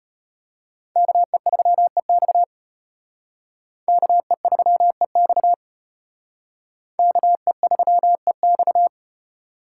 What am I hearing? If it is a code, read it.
Morse code, heard as a keyed tone, about 32 wpm, 700 Hz. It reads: KE3EX KE3EX KE3EX